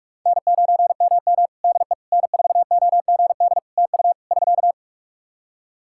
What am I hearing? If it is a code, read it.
Morse code, heard as a keyed tone, about 46 wpm, 700 Hz. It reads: N9MM DE N4OGD TU <SK>
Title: K1USN SST QSO Practice 45wpm